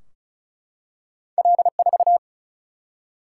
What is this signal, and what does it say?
Morse code, heard as a keyed tone, about 36 wpm, 700 Hz. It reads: L4
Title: Single-Letter Single-Number 35wpm